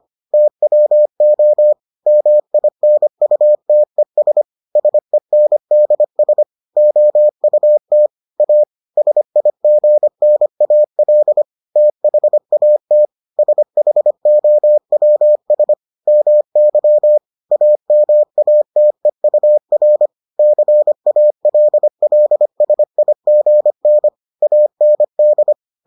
Morse code, heard as a keyed tone, about 25 wpm, 600 Hz. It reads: TWO MINUTES SENDS OUT A SIGNAL THAT SHOWS MY AMATEUR CALLSIGN AND